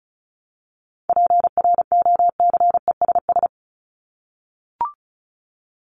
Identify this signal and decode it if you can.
Morse code, heard as a keyed tone, about 34 wpm, 700 Hz. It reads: PROCESS
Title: ING Suffix - Top 100 Words with Base Words 35wpm